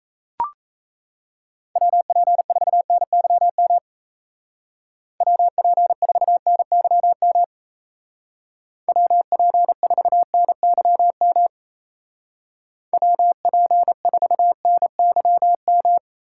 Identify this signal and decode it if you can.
Morse code, heard as a keyed tone, about 34 wpm, 700 Hz. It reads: WP4NYM WP4NYM WP4NYM WP4NYM
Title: Speed Racing - USA Call Signs 28wpm